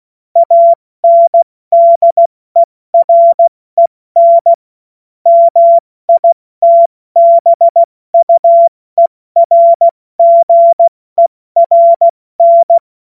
Morse code, heard as a keyed tone, about 16 wpm, 680 Hz. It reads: ANDEREN MITBUERGERN
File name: Morsekurs / Witze / Witz 29. Juni 2021, 70 F80